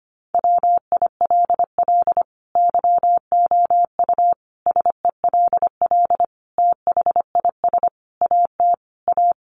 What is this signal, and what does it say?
Morse code, heard as a keyed tone, about 25 wpm, 700 Hz. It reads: WILL YOU SELL THIS AT A
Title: Sentences from Top 1200 Words 25wpm